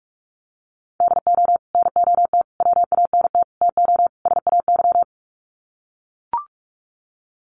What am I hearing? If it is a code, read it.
Morse code, heard as a keyed tone, about 44 wpm, 700 Hz. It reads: DO NOT WANT TO SAY
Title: Sets of 5 Words Encore 45wpm